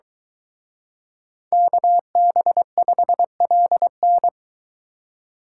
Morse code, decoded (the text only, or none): KB5LN